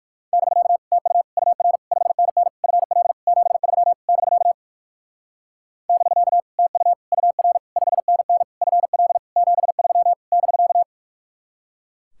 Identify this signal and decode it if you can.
Morse code, heard as a keyed tone, about 59 wpm, 700 Hz. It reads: <BK> TU UR 5NN FL 73 <BK> <BK> TU UR 5NN FL 73 <BK>